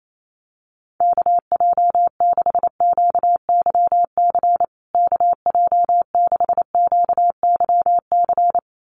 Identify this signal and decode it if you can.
Morse code, heard as a keyed tone, about 28 wpm, 700 Hz. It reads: KJ6QYC KJ6QYC
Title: USA Call Signs Repeated 2x 28wpm